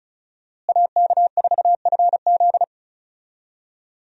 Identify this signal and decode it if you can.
Morse code, heard as a keyed tone, about 34 wpm, 700 Hz. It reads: AK4FZ
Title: USA Call Signs - 5 Characters - Repeated 2x - 35wpm